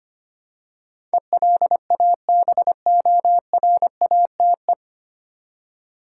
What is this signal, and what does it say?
Morse code, heard as a keyed tone, about 25 wpm, 690 Hz. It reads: ELABORATE
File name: practice / 25wpm / 9 Letter Words - First 3 Letters Given - 25wpm